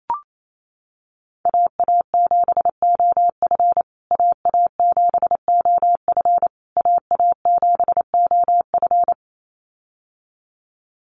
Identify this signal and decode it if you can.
Morse code, heard as a keyed tone, about 28 wpm, 700 Hz. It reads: AA7OF AA7OF AA7OF
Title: USA Call Signs Repeated 3x 28wpm